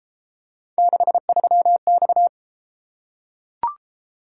Morse code, heard as a keyed tone, about 31 wpm, 700 Hz. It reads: 63X